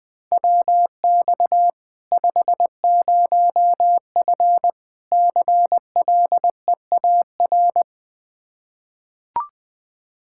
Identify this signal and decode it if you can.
Morse code, heard as a keyed tone, about 20 wpm, 700 Hz. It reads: WX 50F CLEAR
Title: Practice QSO - Exchange 2 - Info - 20wpm